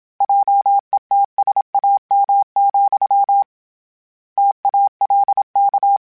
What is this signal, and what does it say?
Morse code, heard as a keyed tone, about 27 wpm, 800 Hz. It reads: JETSAM, TALK
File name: Misc / Misc-mp3s / Word Pairs @25